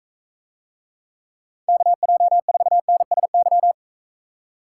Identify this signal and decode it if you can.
Morse code, heard as a keyed tone, about 43 wpm, 700 Hz. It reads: KJ4NSY